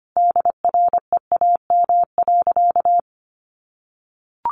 Morse code, heard as a keyed tone, about 25 wpm, 700 Hz. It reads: DREAM.
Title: Sentences from Top 1500 Words 25wpm